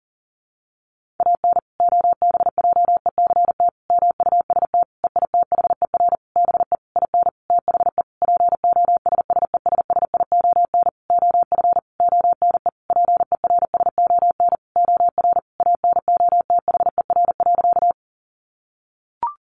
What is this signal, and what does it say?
Morse code, heard as a keyed tone, about 40 wpm, 700 Hz. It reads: AN OBJECT MUST EITHER BE IN THE POSSESSION OF ONE PERSON OR ANOTHER.
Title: Sentences from Top 1400 Words 40wpm